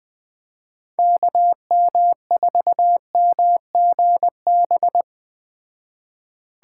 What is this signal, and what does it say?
Morse code, heard as a keyed tone, about 21 wpm, 700 Hz. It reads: KM4MGB